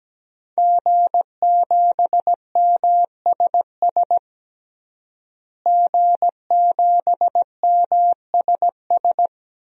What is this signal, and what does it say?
Morse code, heard as a keyed tone, about 17 wpm, 700 Hz. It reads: G7MSS G7MSS
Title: DXCC-100 Call Signs - 5 Characters - Repeated 2x - 17wpm